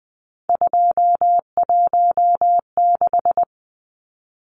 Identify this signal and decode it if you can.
Morse code, heard as a keyed tone, about 20 wpm, 700 Hz. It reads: 216
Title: Numbers - 3-Digit 20wpm